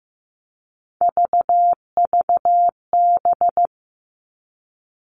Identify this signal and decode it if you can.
Morse code, heard as a keyed tone, about 15 wpm, 700 Hz. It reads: VVB
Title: Three Challenging Letter-Number Combinations 15wpm